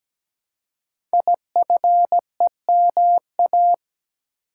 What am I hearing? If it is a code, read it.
Morse code, heard as a keyed tone, about 17 wpm, 700 Hz. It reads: IFEMA
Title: Code Groups - 5 Letter:Numbers - 17wpm